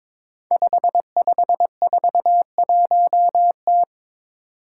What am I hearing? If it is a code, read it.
Morse code, heard as a keyed tone, about 22 wpm, 700 Hz. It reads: 5541T
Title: Five Challenging Letter-Number Combinations 22wpm